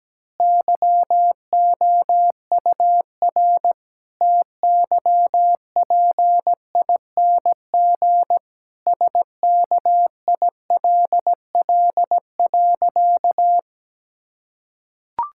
Morse code, heard as a keyed tone, about 17 wpm, 700 Hz. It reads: YOUR TYPING SKILL.
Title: Pangram Challenge v2 - 17wpm